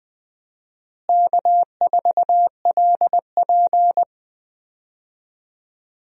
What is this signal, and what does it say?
Morse code, heard as a keyed tone, about 20 wpm, 700 Hz. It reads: K4LP
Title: USA Call Signs - 4 Characters - Repeated 2x - 20wpm